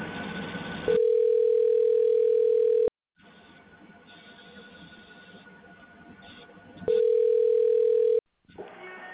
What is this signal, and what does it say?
Ringback tone (North America), 440 + 480 Hz.